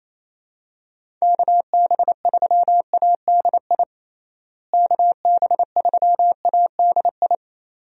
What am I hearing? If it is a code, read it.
Morse code, heard as a keyed tone, about 28 wpm, 700 Hz. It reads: KB3ADI KB3ADI
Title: USA Call Signs - 6 Characters - Repeated 2x - 28wpm